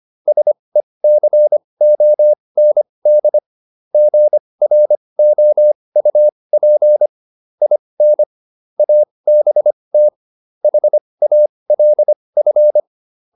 Morse code, heard as a keyed tone, about 25 wpm, spaced as for 20 wpm, 600 Hz. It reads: SECOND GROUP IN ABT HALF